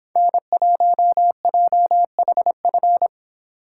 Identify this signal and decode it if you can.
Morse code, heard as a keyed tone, about 26 wpm, 700 Hz. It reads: N1JHF
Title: Speed Racing - USA Call Signs 22wpm